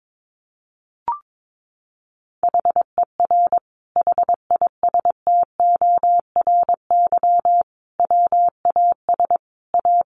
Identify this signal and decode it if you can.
Morse code, heard as a keyed tone, about 22 wpm, 700 Hz. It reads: HER HISTORY WAS A
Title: Sentences from Top 1300 Words 22wpm